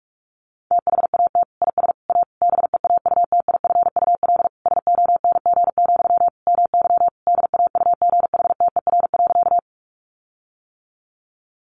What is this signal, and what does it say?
Morse code, heard as a keyed tone, about 44 wpm, 700 Hz. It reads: THAT IS A BEAUTIFUL SONG, MY DAUGHTER.
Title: Sentences from Top 1500 Words 45wpm